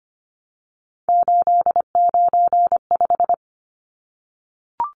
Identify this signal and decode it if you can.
Morse code, heard as a keyed tone, about 25 wpm, 700 Hz. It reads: 895
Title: Numbers - 3-Digit 25wpm